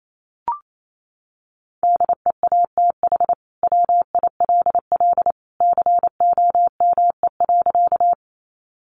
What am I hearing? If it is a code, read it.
Morse code, heard as a keyed tone, about 28 wpm, 700 Hz. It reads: DEATH WILL COME.
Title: Sentences from Top 1100 Words 28wpm